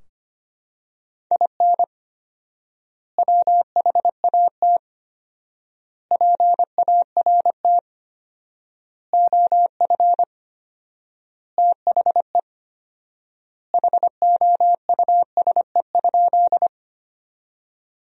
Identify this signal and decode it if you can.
Morse code, heard as a keyed tone, about 25 wpm, 700 Hz. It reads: IN WHAT PART OF THE HOUSE?
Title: 4x Word-Spacing - Sentences from Top 200 Words - 25wpm